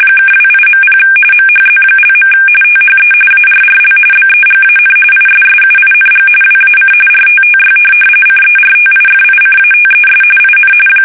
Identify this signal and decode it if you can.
RTTY, about 45.45 baud. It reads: (28.0 MHZ). THE ARRL BEGAN ISSUING WAC RTTY CERTIFICATES IN
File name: fsk850 / teletype-RTTY